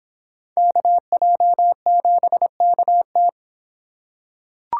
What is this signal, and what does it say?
Morse code, heard as a keyed tone, about 26 wpm, 700 Hz. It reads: KJ7KT